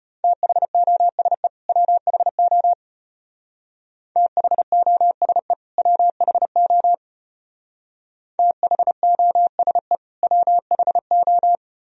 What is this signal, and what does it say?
Morse code, heard as a keyed tone, about 31 wpm, 700 Hz. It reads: THOSE WHO THOSE WHO THOSE WHO
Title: Speed-Racing - Sets of 2 Words 25wpm